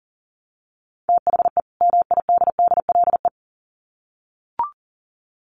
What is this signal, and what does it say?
Morse code, heard as a keyed tone, about 40 wpm, 700 Hz. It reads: THE MIDDLE
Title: Sets of 2 Words 40wpm